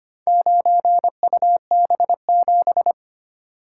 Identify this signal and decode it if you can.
Morse code, heard as a keyed tone, about 25 wpm, 700 Hz. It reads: 9UB7
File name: practice / 25wpm / Four Letter-Number Combinations - 25wpm